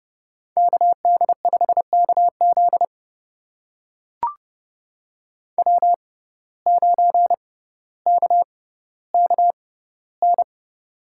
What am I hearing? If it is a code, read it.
Morse code, heard as a keyed tone, about 30 wpm, spaced as for 15 wpm, 700 Hz. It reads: KD5KZ W9KKN
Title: USA Call Signs - 5 Characters - 6x Character Spacing - 30wpm